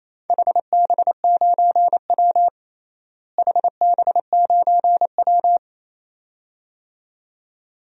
Morse code, heard as a keyed tone, about 28 wpm, 700 Hz. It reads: HB9W HB9W
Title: DXCC-100 Call Signs - 4 Characters - Repeated 2x - 28wpm